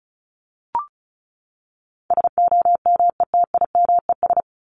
Morse code, heard as a keyed tone, about 34 wpm, 700 Hz. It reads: SOMETIMES
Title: ES:S Suffix - Top 100 Words with Base Words 35wpm